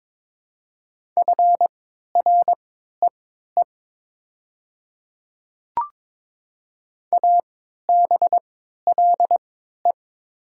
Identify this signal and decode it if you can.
Morse code, heard as a keyed tone, about 22 wpm, spaced as for 12 wpm, 690 Hz. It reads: FREE ABLE
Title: 3x Character Spacing - Top 500 Words - 22wpm